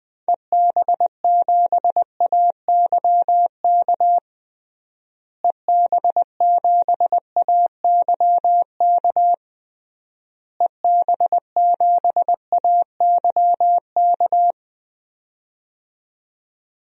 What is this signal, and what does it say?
Morse code, heard as a keyed tone, about 20 wpm, 700 Hz. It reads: EB7AYK EB7AYK EB7AYK
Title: DXCC-100 Call Signs - 6 Characters - Repeated 3x - 20wpm